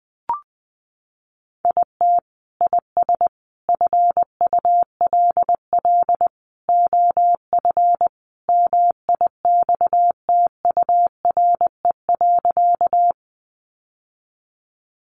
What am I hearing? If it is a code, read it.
Morse code, heard as a keyed tone, about 20 wpm, 700 Hz. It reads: IT IS FULL OF MIXTURE.
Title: Sentences from Top 1500 Words 20wpm